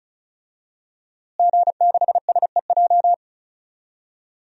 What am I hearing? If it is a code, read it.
Morse code, heard as a keyed tone, about 34 wpm, 700 Hz. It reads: G6SEJ
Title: DXCC-100 Call Signs - 5 Characters - Repeated 2x - 35wpm